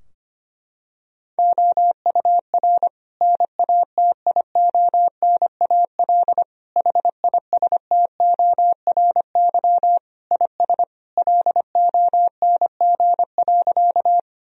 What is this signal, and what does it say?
Morse code, heard as a keyed tone, about 25 wpm, 700 Hz. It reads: OUR NATIONAL HISTORY IS LONG.
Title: Sentences from Top 1000 Words - Review Part 4 - 25wpm